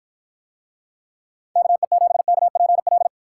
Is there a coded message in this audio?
Yes: Morse code from 1.6 to 3.1 s, heard as a keyed tone, about 53 wpm, 700 Hz. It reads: KE7KPL